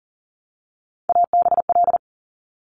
Morse code, heard as a keyed tone, about 40 wpm, 690 Hz. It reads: ABL
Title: Top Three Letter Combinations in Words 40wpm